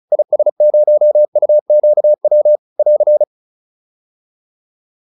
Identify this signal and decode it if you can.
Morse code, heard as a keyed tone, about 34 wpm, 600 Hz. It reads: IS0UQW <AR>
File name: QSO409 35